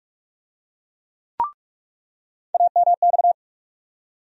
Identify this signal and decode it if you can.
Morse code, heard as a keyed tone, about 46 wpm, 700 Hz. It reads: AMX
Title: USA Call Sign Postfixes - 3 Letters 45wpm